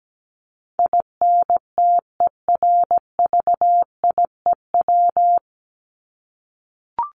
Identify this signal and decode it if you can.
Morse code, heard as a keyed tone, about 17 wpm, 700 Hz. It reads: INTERVIEW
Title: Top 1501-1750 Words 17wpm